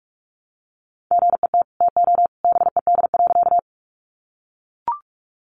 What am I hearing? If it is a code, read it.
Morse code, heard as a keyed tone, about 44 wpm, 700 Hz. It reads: GET TO BED.
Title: Sentences from Top 500 Words 45wpm